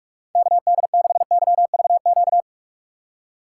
Morse code, heard as a keyed tone, about 44 wpm, 700 Hz. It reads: KD6YVQ
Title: USA Call Signs - 6 Characters - 2x Character Spacing - 45wpm